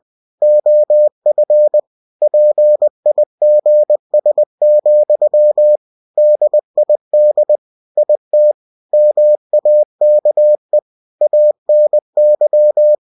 Morse code, heard as a keyed tone, about 20 wpm, 600 Hz. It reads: OF PIGS, DID IT MAKE ANY
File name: cw-books / winnie-the-pooh / winnie-20wpm-page-0041